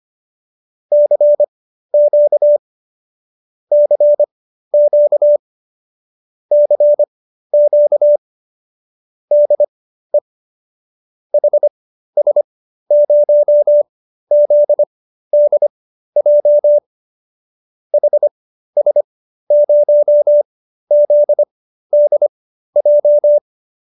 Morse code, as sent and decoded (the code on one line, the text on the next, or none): -.-. --.- / -.-. --.- / -.-. --.- / -.. / .... ... ----- --.. -.. .--- / .... ... ----- --.. -.. .---
CQ CQ CQ DE HS0ZDJ HS0ZDJ